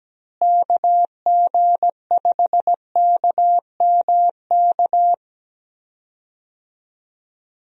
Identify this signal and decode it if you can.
Morse code, heard as a keyed tone, about 17 wpm, 700 Hz. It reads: KG5KMK
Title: USA Call Signs - 6 Characters - Repeated 2x - 17wpm